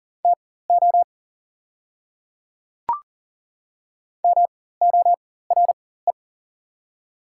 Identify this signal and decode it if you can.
Morse code, heard as a keyed tone, about 40 wpm, spaced as for 19 wpm, 690 Hz. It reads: TO MORE
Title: 4x Character Spacing - Top 100 Words - 40wpm